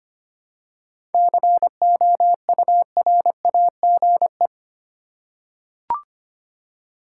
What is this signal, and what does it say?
Morse code, heard as a keyed tone, about 25 wpm, 700 Hz. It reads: COURAGE
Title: Seven Letter Words - 25wpm